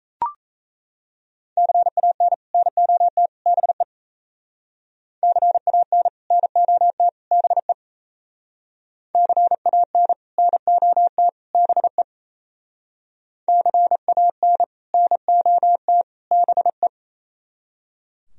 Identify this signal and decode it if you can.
Morse code, heard as a keyed tone, about 37 wpm, 700 Hz. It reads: CAN NOT BE CAN NOT BE CAN NOT BE CAN NOT BE